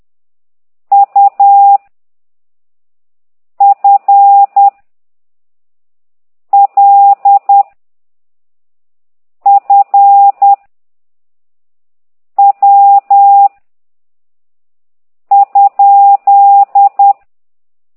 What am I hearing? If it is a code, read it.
Morse code, heard as a keyed tone, about 10 wpm, 800 Hz. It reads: U F L F W ?
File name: ct2idw / exerc8